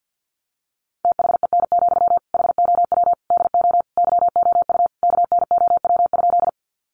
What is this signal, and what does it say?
Morse code, heard as a keyed tone, about 48 wpm, 700 Hz. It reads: THEN, HOW DO YOU KNOW?
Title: Sentences from Top 100 Words 50wpm